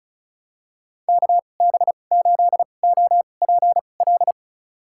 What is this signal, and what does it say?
Morse code, heard as a keyed tone, about 36 wpm, spaced as for 25 wpm, 700 Hz. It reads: KB8OPL
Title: USA Call Signs - 6 Characters - 2x Character Spacing - 35wpm